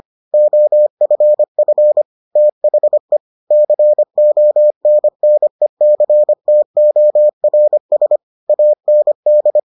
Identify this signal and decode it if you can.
Morse code, heard as a keyed tone, about 25 wpm, 600 Hz. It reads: OFF THE CONNECTORS AND